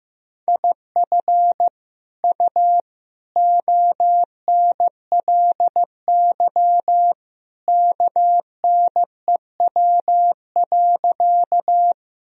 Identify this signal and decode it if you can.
Morse code, heard as a keyed tone, about 15 wpm, 700 Hz. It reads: IF U ONLY KNEW.
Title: Sentences from Top 400 Words and Abbreviations 15wpm